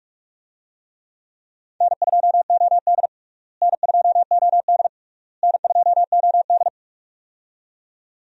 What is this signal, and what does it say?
Morse code, heard as a keyed tone, about 46 wpm, 700 Hz. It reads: N2OD N2OD N2OD
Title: USA Call Signs - 4 Characters - Repeated 3x - 45wpm